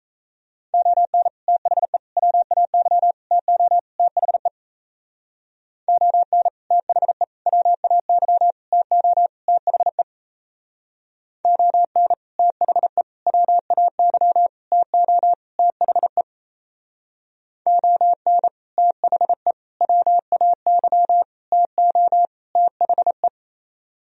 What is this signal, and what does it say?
Morse code, heard as a keyed tone, about 35 wpm, 700 Hz. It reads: ON THE WAY TO THE ON THE WAY TO THE ON THE WAY TO THE ON THE WAY TO THE
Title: Speed-Racing - Sets of 5 Words 28wpm